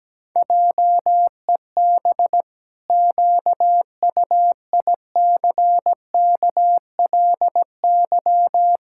Morse code, heard as a keyed tone, about 17 wpm, 700 Hz. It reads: JEB QUICKLY
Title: Pangram Challenge v2 - 17wpm